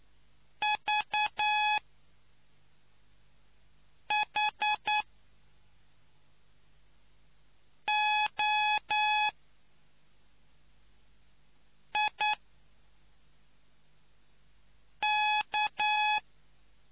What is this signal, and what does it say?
Morse code, heard as a keyed tone, about 9 wpm, spaced as for 3 wpm, 850 Hz. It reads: VHOIK